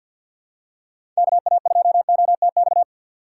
Morse code, heard as a keyed tone, about 53 wpm, 700 Hz. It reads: KA2OTX
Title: USA Call Signs - 6 Characters - 4x Character Spacing - 50wpm